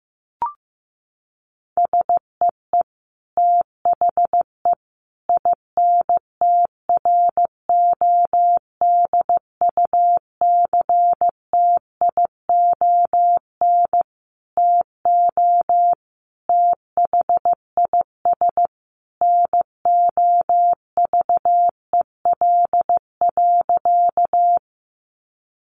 Morse code, heard as a keyed tone, about 15 wpm, 700 Hz. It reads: SEE THE INTRODUCTION TO THIS NOVEL.
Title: Sentences from Top 1300 Words 15wpm